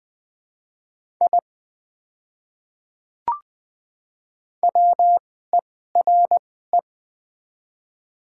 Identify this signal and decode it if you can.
Morse code, heard as a keyed tone, about 20 wpm, spaced as for 14 wpm, 700 Hz. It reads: I WERE
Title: 2x Character Spacing - Top 100 Words - 20wpm